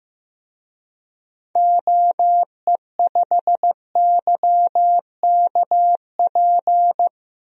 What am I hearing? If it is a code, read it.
Morse code, heard as a keyed tone, about 16 wpm, 700 Hz. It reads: OE5YKP
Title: DXCC-100 Call Signs - 6 Characters - Repeated 2x - 15wpm